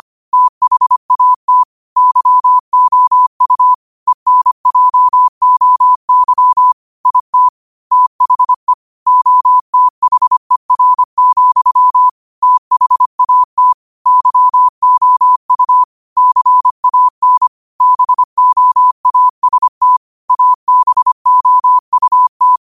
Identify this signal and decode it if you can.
Morse code, heard as a keyed tone, about 25 wpm, 1000 Hz. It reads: THAT YOU ENJOY IT THE OTHER, THAT YOU CAN BOAST ABOUT
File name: qotdfiles / 25wpm